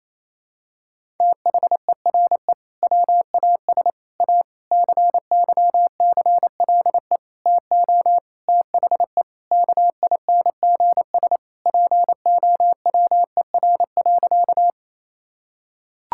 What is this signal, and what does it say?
Morse code, heard as a keyed tone, about 28 wpm, 700 Hz. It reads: THERE WAS A CYCLE TO THE KINGS POWER.
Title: Sentences from Top 1000 Words - Review Part 6 - 28wpm